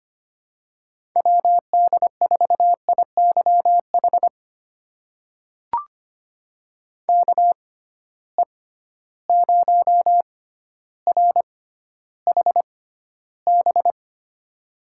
Morse code, heard as a keyed tone, about 25 wpm, spaced as for 13 wpm, 700 Hz. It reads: WD4IYH KE0RHB
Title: USA Call Signs - 6 Characters - 6x Character Spacing - 25wpm